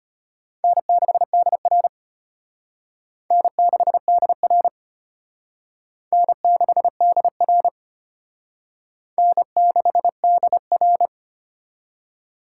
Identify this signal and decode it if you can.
Morse code, heard as a keyed tone, about 30 wpm, 700 Hz. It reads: N6DR N6DR N6DR N6DR
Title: Speed Racing - USA 1x2 Call Signs 25wpm